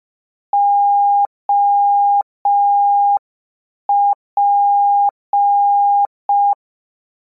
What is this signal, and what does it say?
Morse code, heard as a keyed tone, about 5 wpm, 810 Hz. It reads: OP